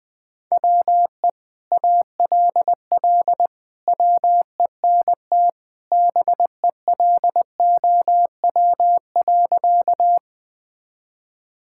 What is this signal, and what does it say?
Morse code, heard as a keyed tone, about 20 wpm, 700 Hz. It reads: WE ALL WENT BELOW.
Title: Sentences from Top 1000 Words - Review Part 5 - 20wpm